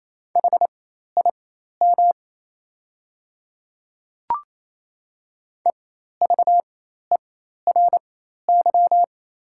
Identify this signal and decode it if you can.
Morse code, heard as a keyed tone, about 28 wpm, spaced as for 13 wpm, 700 Hz. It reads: HIM EVERY
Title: 4x Character Spacing - Top 200 Words - 28wpm